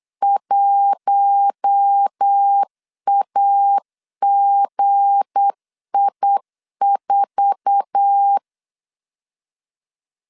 Morse code, heard as a keyed tone, about 8 wpm, 800 Hz. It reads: 1AGI4
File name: LEZIONI / 8 YJ45 / 8 YJ45 8wpm 3214